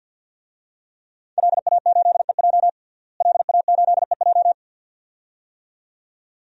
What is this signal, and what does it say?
Morse code, heard as a keyed tone, about 51 wpm, 700 Hz. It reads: LA8EJ LA8EJ
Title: DXCC-100 Call Signs - 5 Characters - Repeated 2x - 50wpm